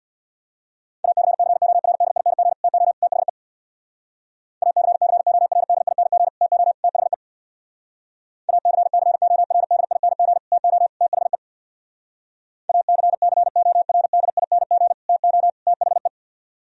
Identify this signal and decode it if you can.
Morse code, heard as a keyed tone, about 60 wpm, 700 Hz. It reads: ACCORDING TO THE ACCORDING TO THE ACCORDING TO THE ACCORDING TO THE
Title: Speed-Racing - Sets of 3 Words 50wpm